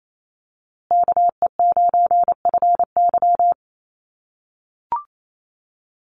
Morse code, heard as a keyed tone, about 28 wpm, 700 Hz. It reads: KE9FY
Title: USA Call Signs Repeated 2x 28wpm